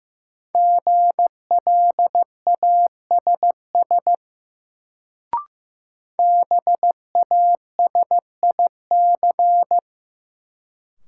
Morse code, heard as a keyed tone, about 15 wpm, 700 Hz. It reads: GLASS BASIC
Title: Sentences from Top 1000 Words - Review Part 3 - New Word Drill - 15wpm